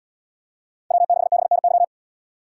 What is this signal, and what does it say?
Morse code, heard as a keyed tone, about 71 wpm, 700 Hz. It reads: W6BNY